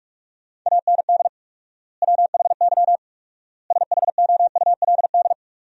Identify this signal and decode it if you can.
Morse code, heard as a keyed tone, about 46 wpm, 700 Hz. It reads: AND WHY SHOULD
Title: 4x Word-Spacing - Sentences from Top 300 Words - 45wpm